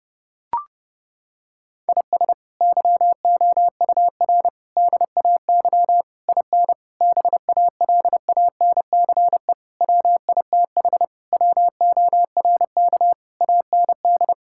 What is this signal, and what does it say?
Morse code, heard as a keyed tone, about 30 wpm, 700 Hz. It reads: IS YOUR DAY IN BALANCE WITH WORK AND
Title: Sentences from Top 1000 Words - Review Part 6 - 30wpm